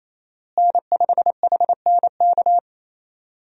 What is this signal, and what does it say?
Morse code, heard as a keyed tone, about 27 wpm, 700 Hz. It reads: N5HNK
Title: USA Call Signs - 5 Characters - Repeated 2x - 28wpm